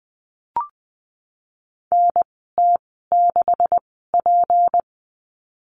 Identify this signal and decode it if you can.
Morse code, heard as a keyed tone, about 20 wpm, spaced as for 15 wpm, 700 Hz. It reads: NT6P
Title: 154 Lesson 25g - Letter p - Call Signs 20wpm